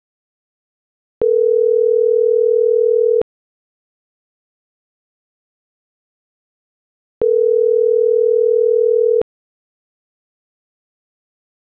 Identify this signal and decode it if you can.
Ringback tone (North America), 440 + 480 Hz.